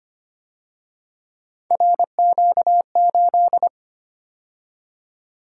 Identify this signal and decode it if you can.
Morse code, heard as a keyed tone, about 31 wpm, 700 Hz. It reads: RQ8
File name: Speed-Racing - Three Letter-Number Combinations - 25wpm